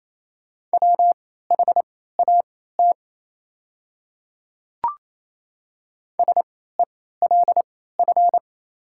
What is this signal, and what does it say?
Morse code, heard as a keyed tone, about 28 wpm, spaced as for 16 wpm, 700 Hz. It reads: WHAT SELF
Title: 3x Character Spacing - Top 300 Words - 28wpm